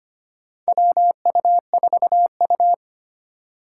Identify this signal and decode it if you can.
Morse code, heard as a keyed tone, about 25 wpm, 700 Hz. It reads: WU4U
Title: USA Call Signs - 4 Characters - 2x Character Spacing - 25wpm